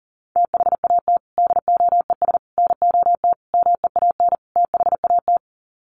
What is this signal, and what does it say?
Morse code, heard as a keyed tone, about 40 wpm, 700 Hz. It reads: THAT DOES NOT MEAN THAT
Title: Sets of 5 Words Encore 40wpm